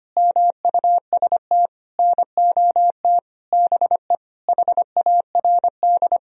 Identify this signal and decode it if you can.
Morse code, heard as a keyed tone, about 25 wpm, 700 Hz. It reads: MUST NOT BE HARD
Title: Sentences from Top 1000 Words - Review Part 1 - 25wpm